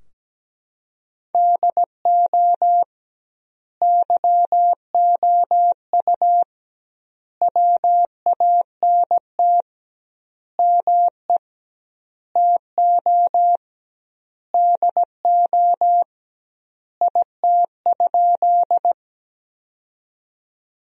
Morse code, heard as a keyed tone, about 17 wpm, 700 Hz. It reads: DO YOU WANT ME TO DO IT?
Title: 2x Word-Spacing - Sentences from Top 200 Words - 17wpm